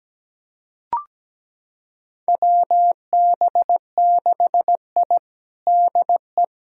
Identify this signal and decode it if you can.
Morse code, heard as a keyed tone, about 17 wpm, 700 Hz. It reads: WB6I DE
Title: USA Callsign DE Callsign - 4 Characters - 17wpm